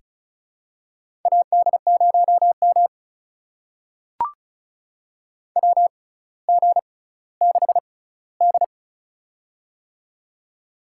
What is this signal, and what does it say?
Morse code, heard as a keyed tone, about 36 wpm, spaced as for 18 wpm, 700 Hz. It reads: AD0M WG6D
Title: USA Call Signs - 4 Characters - 6x Character Spacing - 35wpm